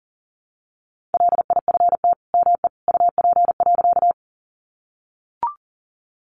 Morse code, heard as a keyed tone, about 40 wpm, 700 Hz. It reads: LIFT ME UP.
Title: Sentences from Top 1000 Words 40wpm